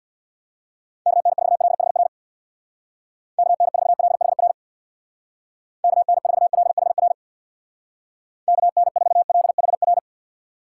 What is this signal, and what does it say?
Morse code, heard as a keyed tone, about 61 wpm, 700 Hz. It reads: KN4LHR KN4LHR KN4LHR KN4LHR
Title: Speed Racing - USA Call Signs 50wpm